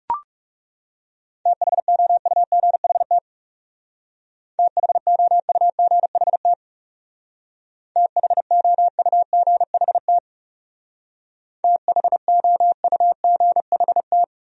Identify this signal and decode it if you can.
Morse code, heard as a keyed tone, about 36 wpm, 700 Hz. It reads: THOUGHT THOUGHT THOUGHT THOUGHT
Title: Speed-Racing Top 500 Words 30wpm